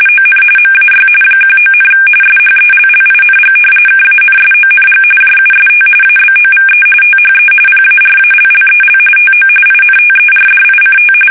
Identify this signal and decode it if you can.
RTTY, about 45.45 baud. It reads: ON MAY 19, SHELLING KANSONG JUST ABOVE THE PARALLEL TO INTERDICT
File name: fsk850 / battleshipnj